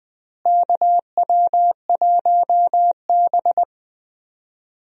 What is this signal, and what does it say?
Morse code, heard as a keyed tone, about 20 wpm, 700 Hz. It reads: KW1B
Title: USA Call Signs - 4 Characters - Repeated 3x - 20wpm